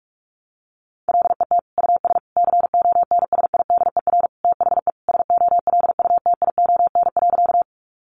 Morse code, heard as a keyed tone, about 44 wpm, 700 Hz. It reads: LET US CONSIDER THE SOLUTION.
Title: Sentences from Top 1300 Words 45wpm